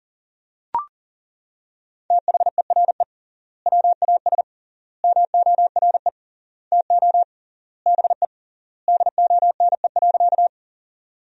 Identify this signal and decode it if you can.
Morse code, heard as a keyed tone, about 40 wpm, 700 Hz. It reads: THERE WAS MORE TO BE DONE.
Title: 3x Word-Spacing - Sentences from Top 500 Words - 40wpm